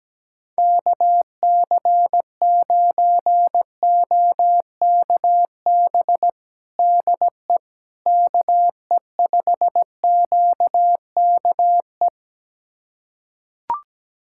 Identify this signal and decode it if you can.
Morse code, heard as a keyed tone, about 17 wpm, 700 Hz. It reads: KC9OKB DE KE5QKE
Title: USA Callsign DE Callsign - 6 Characters - 17wpm